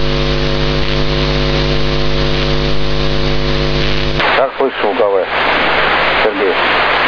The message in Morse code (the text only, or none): none